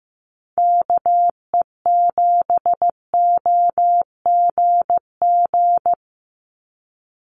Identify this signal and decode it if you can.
Morse code, heard as a keyed tone, about 15 wpm, 700 Hz. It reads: KE7OGG
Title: USA Call Signs 15wpm